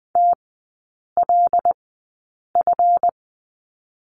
Morse code, heard as a keyed tone, about 20 wpm, spaced as for 8 wpm, 700 Hz. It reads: TLF
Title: 152 Lesson 25e - Letter p - Three Characters 20wpm